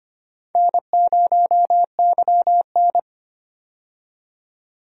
Morse code, heard as a keyed tone, about 25 wpm, 700 Hz. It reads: N0YN
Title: USA Call Signs - 4 Characters - 25wpm